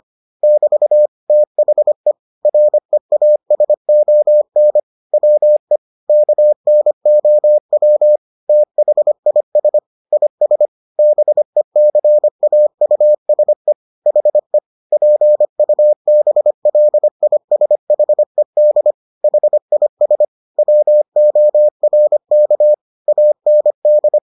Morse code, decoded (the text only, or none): <BT> THE REASON WE KNOW THIS IS BECAUSE HE PUBLISHED HIS WORK AND